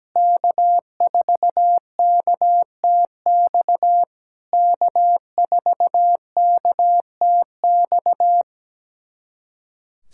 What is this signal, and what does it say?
Morse code, heard as a keyed tone, about 17 wpm, 700 Hz. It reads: K4KTX K4KTX
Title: SOTA QSO Practice 17wpm